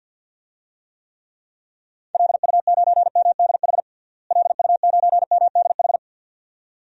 Morse code, heard as a keyed tone, about 49 wpm, 700 Hz. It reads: LU9MDH LU9MDH
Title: DXCC-100 Call Signs - 6 Characters - Repeated 2x - 50wpm